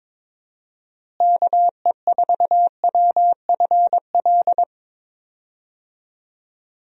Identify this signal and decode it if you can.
Morse code, heard as a keyed tone, about 22 wpm, 700 Hz. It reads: KE4WFL
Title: USA Call Signs - 6 Characters - Repeated 3x - 22wpm